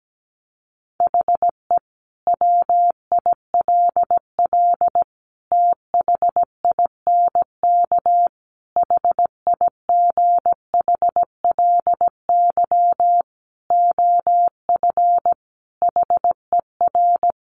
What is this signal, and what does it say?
Morse code, heard as a keyed tone, about 17 wpm, 700 Hz. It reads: HE WILL THINK HIGHLY OF HER
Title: Sentences from Top 1100 Words 17wpm